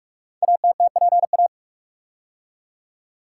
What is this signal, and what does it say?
Morse code, heard as a keyed tone, about 44 wpm, 700 Hz. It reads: ATTPA
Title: Code Groups - 5 Letters - 45wpm